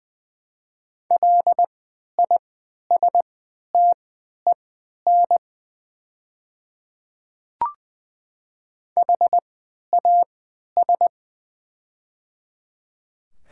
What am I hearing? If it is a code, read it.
Morse code, heard as a keyed tone, about 20 wpm, spaced as for 11 wpm, 700 Hz. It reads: LISTEN HAS
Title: 3x Character Spacing - Top 500 Words - 20wpm